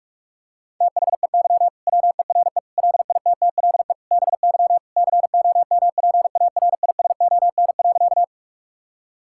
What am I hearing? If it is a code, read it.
Morse code, heard as a keyed tone, about 41 wpm, 700 Hz. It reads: THEY WERE LITTLE BY COMPARISON.
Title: Sentences from Top 1600 Words 45wpm